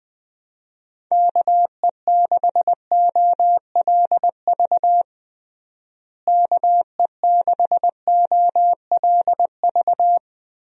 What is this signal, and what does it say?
Morse code, heard as a keyed tone, about 20 wpm, 700 Hz. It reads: KE6OLV KE6OLV
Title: USA Call Signs - 6 Characters - Repeated 3x - 20wpm